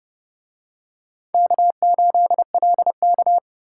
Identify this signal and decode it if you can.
Morse code, heard as a keyed tone, about 29 wpm, 700 Hz. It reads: K8LK